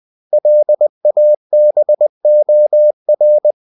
Morse code, heard as a keyed tone, about 20 wpm, 600 Hz. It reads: LABOR
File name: CW / Practice / 500MostCommonEnglishWords20